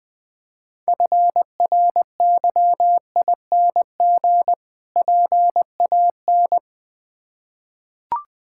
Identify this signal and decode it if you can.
Morse code, heard as a keyed tone, about 20 wpm, 700 Hz. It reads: FRYING PAN
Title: Difficult Words with Hints - 20wpm